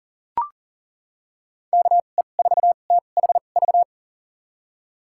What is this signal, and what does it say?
Morse code, heard as a keyed tone, about 40 wpm, spaced as for 29 wpm, 700 Hz. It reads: KE4THV
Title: USA Call Signs - 6 Characters - 2x Character Spacing - 40wpm